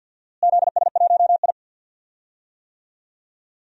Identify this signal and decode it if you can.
Morse code, heard as a keyed tone, about 52 wpm, 700 Hz. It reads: ZS1I
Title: DXCC-100 Call Signs - 4 Characters - Repeated 2x - 50wpm